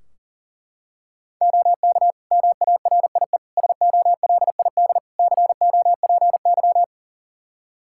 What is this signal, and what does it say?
Morse code, heard as a keyed tone, about 40 wpm, 700 Hz. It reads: OK MARIE SOLID COPY
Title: Practice QSO - Exchange 3 - More Info - 40wpm